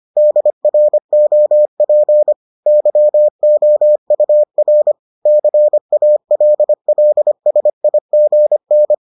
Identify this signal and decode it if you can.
Morse code, heard as a keyed tone, about 25 wpm, 600 Hz. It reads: DROP YOUR CALLSIGN